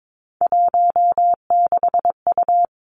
Morse code, heard as a keyed tone, about 22 wpm, 700 Hz. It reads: 16U
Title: Three Challenging Letter-Number Combinations 22wpm